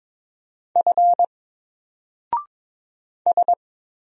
Morse code, heard as a keyed tone, about 22 wpm, 700 Hz. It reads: F S
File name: Single-Letter-Number-22wpm